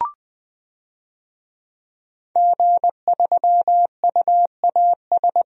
Morse code, heard as a keyed tone, about 20 wpm, 700 Hz. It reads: G3UAS
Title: DXCC-100 Call Signs - 5 Characters - 20wpm